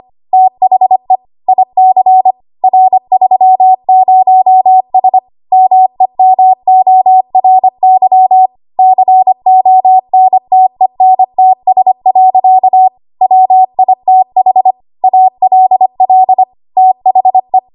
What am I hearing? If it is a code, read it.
Morse code, heard as a keyed tone, about 25 wpm, 750 Hz. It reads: THE IC R30S MEMORY CONTENTS. WITH ALL THE
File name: Archive / 25 WPM / 25WPM